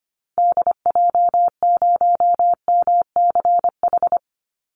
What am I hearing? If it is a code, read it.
Morse code, heard as a keyed tone, about 25 wpm, 700 Hz. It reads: DJ0MCH
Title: Call Signs Repeated 3x - 25wpm